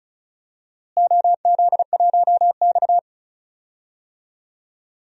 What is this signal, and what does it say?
Morse code, heard as a keyed tone, about 34 wpm, 700 Hz. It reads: OZ1X
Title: DXCC-100 Call Signs - 4 Characters - Repeated 2x - 35wpm